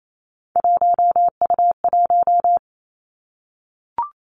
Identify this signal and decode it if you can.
Morse code, heard as a keyed tone, about 28 wpm, 700 Hz. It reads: 1U1
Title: Three Challenging Letter-Number Combinations 28wpm